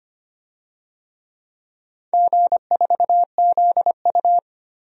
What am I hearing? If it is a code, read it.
Morse code, heard as a keyed tone, about 25 wpm, 700 Hz. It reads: G4ZU
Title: DXCC-100 Call Signs - 4 Characters - 25wpm